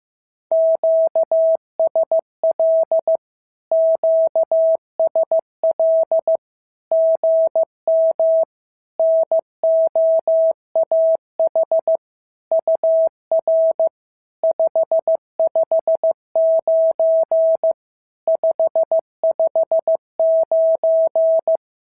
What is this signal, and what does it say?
Morse code, heard as a keyed tone, about 15 wpm, 650 Hz. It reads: QSL QSL GM NOAH UR 559 559
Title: SOTA QSO Practice 15wpm